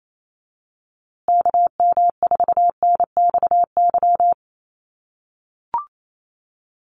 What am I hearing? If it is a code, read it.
Morse code, heard as a keyed tone, about 28 wpm, 700 Hz. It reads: KM4NXY
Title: USA Call Signs Repeated 2x 28wpm